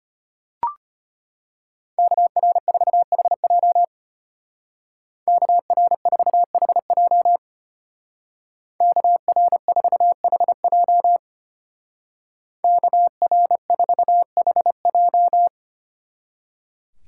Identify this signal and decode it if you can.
Morse code, heard as a keyed tone, about 31 wpm, 700 Hz. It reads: KR4HJ KR4HJ KR4HJ KR4HJ
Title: Speed Racing - USA 2x2 Call Signs 25wpm